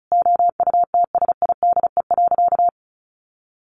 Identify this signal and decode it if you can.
Morse code, heard as a keyed tone, about 34 wpm, 700 Hz. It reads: OUTSIDE.
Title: Sentences from Top 900 Words 35wpm